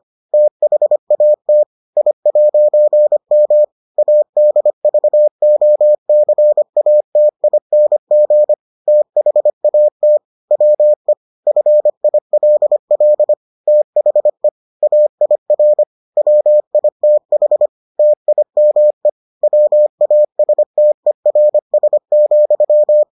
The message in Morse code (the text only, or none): THAT I'M ADVOCATING THAT WE FILL THE AIR WITH TIME WASTERS,